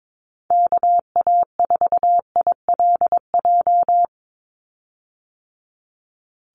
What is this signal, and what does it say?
Morse code, heard as a keyed tone, about 22 wpm, 700 Hz. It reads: KA4ILJ
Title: USA Call Signs 22wpm